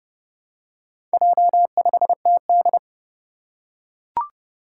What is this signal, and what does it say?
Morse code, heard as a keyed tone, about 30 wpm, 700 Hz. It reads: J5TD
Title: Four Challenging Letter-Number Combinations 30wpm